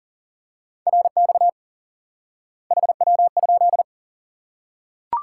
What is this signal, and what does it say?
Morse code, heard as a keyed tone, about 40 wpm, 700 Hz. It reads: RX HW?
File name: IQE Course 3 - Lesson 89 - YAGI - Two Elements - 40wpm